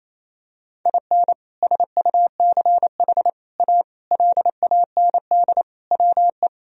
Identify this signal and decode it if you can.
Morse code, heard as a keyed tone, about 28 wpm, 700 Hz. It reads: IN SUCH A LAND WE
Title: Sentences from Top 1000 Words - Review Part 1 - 28wpm